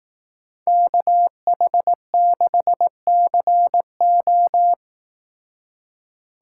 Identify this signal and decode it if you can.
Morse code, heard as a keyed tone, about 18 wpm, 700 Hz. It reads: KH6CO
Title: Speed Racing - USA 2x2 Call Signs 15wpm